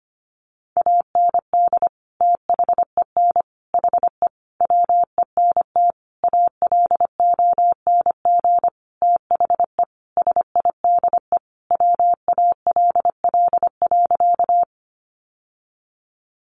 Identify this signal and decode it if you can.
Morse code, heard as a keyed tone, about 25 wpm, 700 Hz. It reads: AND THEN HE WENT ALONG THE SIDE WALL.
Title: Sentences from Top 1100 Words 25wpm